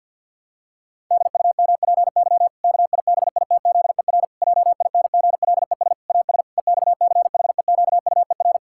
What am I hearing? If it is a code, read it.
Morse code, heard as a keyed tone, about 50 wpm, 700 Hz. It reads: DUMPY KIBITZER JINGLES AS EXCHEQUER
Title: Pangram Challenge v2 - 50wpm